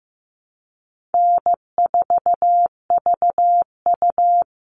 Morse code, heard as a keyed tone, about 15 wpm, 700 Hz. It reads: N4VU
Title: All Cut Numbers - 4-Digit 15wpm